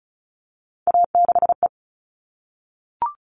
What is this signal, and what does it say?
Morse code, heard as a keyed tone, about 34 wpm, 700 Hz. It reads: A6E
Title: All Cut Numbers - 3-Digit 35wpm